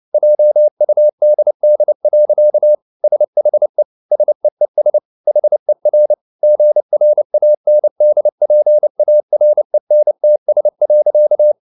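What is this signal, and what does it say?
Morse code, heard as a keyed tone, about 29 wpm, 600 Hz. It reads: JUDD. SHE SEES HER GRANDPARENTS.
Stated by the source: Morse code at 29 wpm, sending Ken Follett's novel Never in English, from CW med Gnister.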